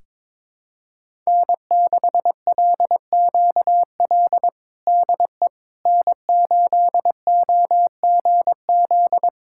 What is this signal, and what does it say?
Morse code, heard as a keyed tone, about 22 wpm, 700 Hz. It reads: N6LQL DE N8OGZ
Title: USA Callsign DE Callsign - 5 Characters - 22wpm